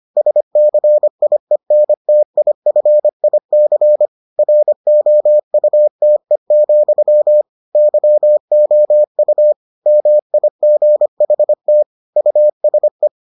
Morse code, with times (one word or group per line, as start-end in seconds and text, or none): 0.2-4.1 SCIENTIFIC
4.4-7.4 ROUTE,
7.8-9.5 YOU
9.9-11.8 MIGHT
12.2-13.1 USE